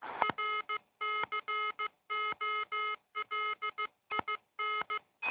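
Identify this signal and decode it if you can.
Morse code, heard as a keyed tone, about 15 wpm, 1280 Hz, with noise. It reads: RCOLIN
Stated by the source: Morse code from GI0RQK, worked on 80 m with low power from the Rozkmit transceiver.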